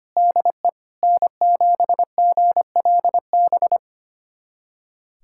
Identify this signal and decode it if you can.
Morse code, heard as a keyed tone, about 25 wpm, 700 Hz. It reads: DE N7GLB
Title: USA Callsign DE Callsign - 5 Characters - 25wpm